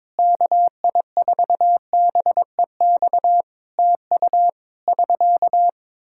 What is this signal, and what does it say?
Morse code, heard as a keyed tone, about 22 wpm, 700 Hz. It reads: KI4BEX TU <SK>
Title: K1USN SST QSO Practice 22wpm